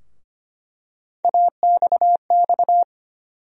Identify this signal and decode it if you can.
Morse code, heard as a keyed tone, about 25 wpm, 700 Hz. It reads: AXX
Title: USA Call Sign Postfixes - 3 Letters 25wpm